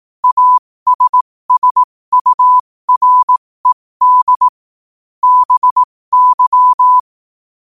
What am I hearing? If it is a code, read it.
Morse code, heard as a keyed tone, about 18 wpm, spaced as for 15 wpm, 1000 Hz. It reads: ASSURED BY
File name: qotdfiles / 15wpm